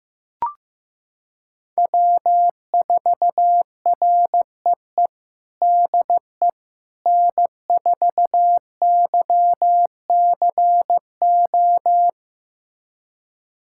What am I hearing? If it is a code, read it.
Morse code, heard as a keyed tone, about 15 wpm, 700 Hz. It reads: W4REE DE N4YCO
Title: USA Callsign DE Callsign - 5 Characters - 15wpm